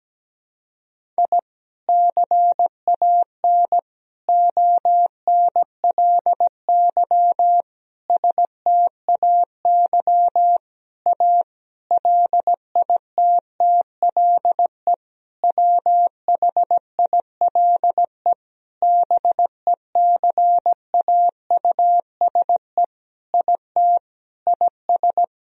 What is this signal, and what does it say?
Morse code, heard as a keyed tone, about 17 wpm, 700 Hz. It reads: I CAN ONLY STAY A LITTLE WHILE BECAUSE IT IS
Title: Sentences from Top 1000 Words - Review Part 3 - 17wpm